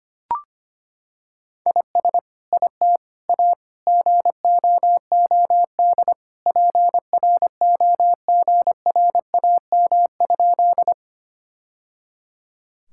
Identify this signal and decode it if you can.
Morse code, heard as a keyed tone, about 25 wpm, 700 Hz. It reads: IS IT A GOOD PROGRAM?
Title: Sentences from Top 1000 Words - Review Part 5 - 25wpm